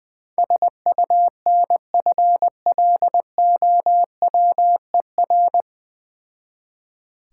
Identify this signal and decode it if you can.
Morse code, heard as a keyed tone, about 20 wpm, 690 Hz. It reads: SUNFLOWER
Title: Difficult Words with Hints - 20wpm